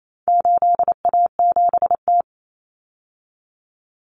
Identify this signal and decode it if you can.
Morse code, heard as a keyed tone, about 28 wpm, 700 Hz. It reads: 8A7T
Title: Common Cut Numbers - 4-Digit 28wpm